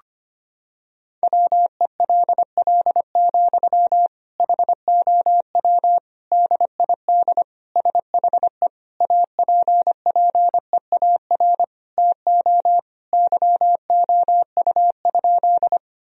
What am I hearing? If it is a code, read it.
Morse code, heard as a keyed tone, about 25 wpm, 700 Hz. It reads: WELL, HOW DID SHE APPEAR TO YOU?
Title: Sentences from Top 1000 Words - Review Part 2 - 25wpm